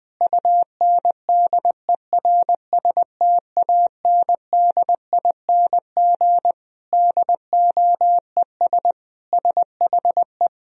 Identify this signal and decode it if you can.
Morse code, heard as a keyed tone, about 20 wpm, 700 Hz. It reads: UNDERSTANDING DOES SHE
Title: Sentences from Top 1000 Words - Review Part 3 - 20wpm